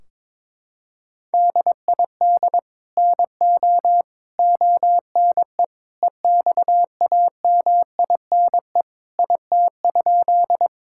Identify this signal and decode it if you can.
Morse code, heard as a keyed tone, about 22 wpm, 700 Hz. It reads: DID NO ONE EXAMINE IT?
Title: Sentences from Top 1600 Words 22wpm